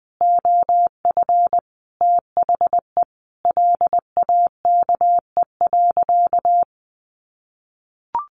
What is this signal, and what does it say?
Morse code, heard as a keyed tone, about 20 wpm, 700 Hz. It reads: OF THE LAKE.
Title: Sentences from Top 1400 Words 20wpm